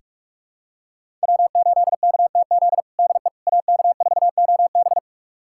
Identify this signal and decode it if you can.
Morse code, heard as a keyed tone, about 46 wpm, 700 Hz. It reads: W8KTZ DE AK4OB
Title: USA Callsign DE Callsign - 5 Characters - 45wpm